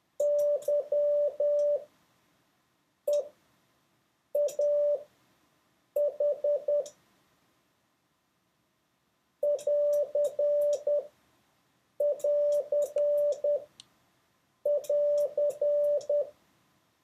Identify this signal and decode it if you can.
Morse code, heard as a keyed tone, about 10 wpm, spaced as for 6 wpm, 580 Hz. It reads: YEAH <AR><AR><AR>
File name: 6of6-test